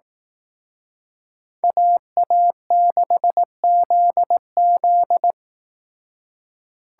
Morse code, heard as a keyed tone, about 18 wpm, 700 Hz. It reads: AA6ZZ
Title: Speed Racing - USA 2x2 Call Signs 15wpm